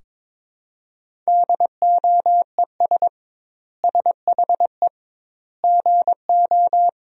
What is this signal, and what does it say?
Morse code, heard as a keyed tone, about 22 wpm, 700 Hz. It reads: DOES SHE GO
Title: 2x Word-Spacing - Sentences from Top 400 Words - 22wpm